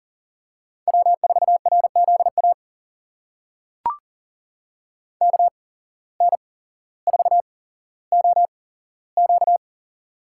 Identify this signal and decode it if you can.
Morse code, heard as a keyed tone, about 40 wpm, spaced as for 17 wpm, 700 Hz. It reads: W4RZA KN4OQ
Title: USA Call Signs - 5 Characters - 8x Character Spacing - 40wpm